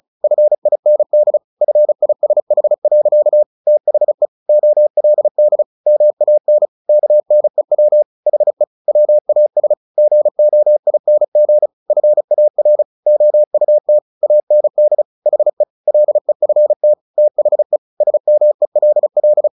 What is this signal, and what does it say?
Morse code, heard as a keyed tone, about 34 wpm, 600 Hz. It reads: FIND FISH. THE OLD MAN KNEW HE WAS GOING FAR OUT AND HE LEFT THE SMELL
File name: the old man and the sea / the old man and the sea 1 35wpm